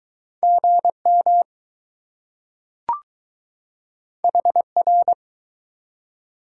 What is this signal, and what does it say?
Morse code, heard as a keyed tone, about 20 wpm, 700 Hz. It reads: GM HR